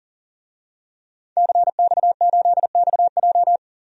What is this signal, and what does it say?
Morse code, heard as a keyed tone, about 39 wpm, 700 Hz. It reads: CX8XJ
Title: DXCC-100 Call Signs - 5 Characters - 40wpm